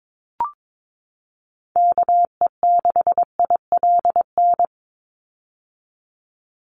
Morse code, heard as a keyed tone, about 22 wpm, 700 Hz. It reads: KE6ILN
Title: USA Call Signs 22wpm